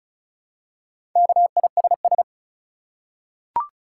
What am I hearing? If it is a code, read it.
Morse code, heard as a keyed tone, about 34 wpm, 700 Hz. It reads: KISS
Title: Four Letter Words - 35wpm